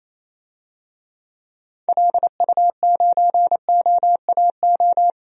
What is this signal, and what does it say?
Morse code, heard as a keyed tone, about 28 wpm, 700 Hz. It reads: LU9OAO
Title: DXCC-100 Call Signs - 6 Characters - 28wpm